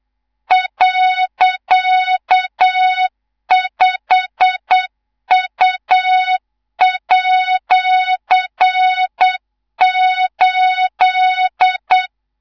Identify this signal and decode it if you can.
Morse code, heard as a keyed tone, about 8 wpm, 3700 Hz, with noise. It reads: .5U@8
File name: Morze-HA2MN / hang / 19 GYAKORLAT